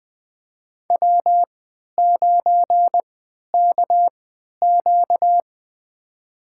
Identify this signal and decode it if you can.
Morse code, heard as a keyed tone, about 20 wpm, spaced as for 11 wpm, 700 Hz. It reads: W9KQ
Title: USA Call Signs - 4 Characters - 3x Character Spacing - 20wpm